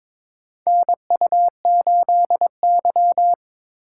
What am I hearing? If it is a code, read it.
Morse code, heard as a keyed tone, about 22 wpm, 700 Hz. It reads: NU8Y
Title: USA Call Signs - 4 Characters - 3x Character Spacing - 22wpm